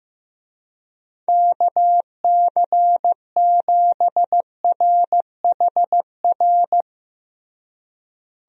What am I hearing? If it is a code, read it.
Morse code, heard as a keyed tone, about 15 wpm, 700 Hz. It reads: KC7RHR
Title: USA Call Signs - 6 Characters - Repeated 3x - 15wpm